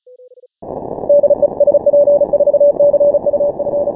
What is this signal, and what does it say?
Morse code, heard as a keyed tone, about 33 wpm, 580 Hz. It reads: DE SP4CUF
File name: final / W1UJ-168-140-MorseRunner